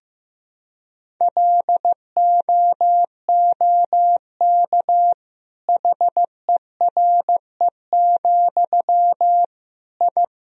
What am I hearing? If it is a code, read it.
Morse code, heard as a keyed tone, about 15 wpm, 700 Hz. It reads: LOOK HERE, I
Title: Sentences from Top 1000 Words - Review Part 1 - 15wpm